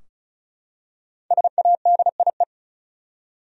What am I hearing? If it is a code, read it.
Morse code, heard as a keyed tone, about 36 wpm, 700 Hz. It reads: SADIE
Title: USA Common Names - Max 5 Letters - 35wpm